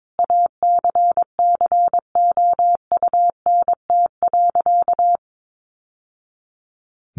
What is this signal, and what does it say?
Morse code, heard as a keyed tone, about 22 wpm, 700 Hz. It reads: ACCOUNT.
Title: Sentences from Top 900 Words 22wpm